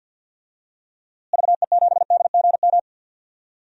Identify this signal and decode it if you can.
Morse code, heard as a keyed tone, about 49 wpm, 700 Hz. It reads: VE7DGM
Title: DXCC-100 Call Signs - 6 Characters - Repeated 2x - 50wpm